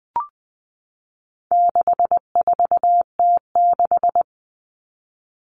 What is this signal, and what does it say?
Morse code, heard as a keyed tone, about 20 wpm, 700 Hz. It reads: 64T6
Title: Common Cut Numbers - 4-Digit 20wpm